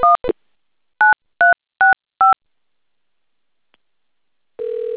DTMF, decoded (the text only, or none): 1 9365